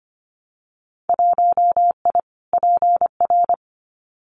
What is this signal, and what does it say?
Morse code, heard as a keyed tone, about 25 wpm, 700 Hz. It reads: 1I PR
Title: Winter Field Day Exchanges Encore 25wpm